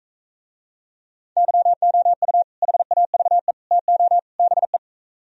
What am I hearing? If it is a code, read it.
Morse code, heard as a keyed tone, about 45 wpm, 700 Hz. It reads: YOU HAVE TO BE